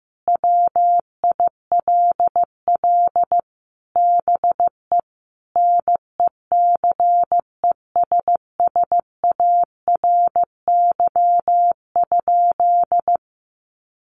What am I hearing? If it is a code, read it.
Morse code, heard as a keyed tone, about 15 wpm, 700 Hz. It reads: WILL BE NECESSARY?
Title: Sentences from Top 800 Words 15wpm